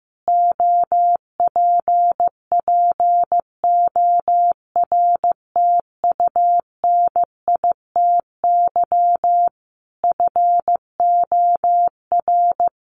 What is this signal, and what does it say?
Morse code, heard as a keyed tone, about 14 wpm, 700 Hz. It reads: OPPORTUNITY FOR
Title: Sentences from Top 1400 Words 15wpm